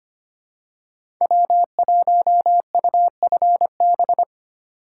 Morse code, heard as a keyed tone, about 25 wpm, 700 Hz. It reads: W1UFB
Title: USA Call Signs - 5 Characters - 2x Character Spacing - 25wpm